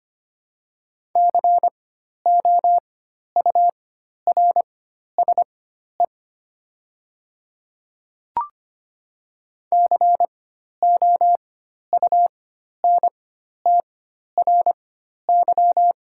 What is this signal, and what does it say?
Morse code, heard as a keyed tone, about 25 wpm, spaced as for 12 wpm, 700 Hz. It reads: COURSE COUNTRY
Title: 4x Character Spacing - Top 500 Words - 25wpm